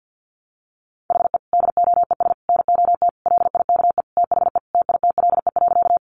Morse code, heard as a keyed tone, about 50 wpm, 700 Hz. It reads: HE DOES NOT LIKE THE TITLE.
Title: Sentences from Top 1200 Words 50wpm